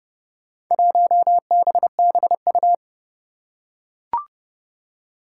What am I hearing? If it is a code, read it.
Morse code, heard as a keyed tone, about 30 wpm, 700 Hz. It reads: 1BBU
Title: Four Challenging Letter-Number Combinations 30wpm